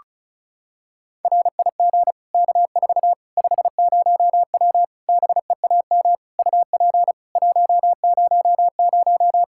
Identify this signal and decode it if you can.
Morse code, heard as a keyed tone, about 36 wpm, 700 Hz. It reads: RIG K4 50W BEAM UP 100
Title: Practice QSO - Exchange 2 - Info - 35wpm